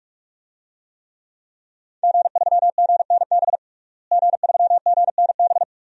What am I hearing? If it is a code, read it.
Morse code, heard as a keyed tone, about 45 wpm, 700 Hz. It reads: G3GNB G3GNB
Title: DXCC-100 Call Signs - 5 Characters - Repeated 3x - 45wpm